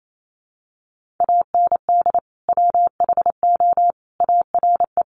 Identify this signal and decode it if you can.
Morse code, heard as a keyed tone, about 28 wpm, 700 Hz. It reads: AND WHO ARE
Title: Sentences from Top 100 Words 28wpm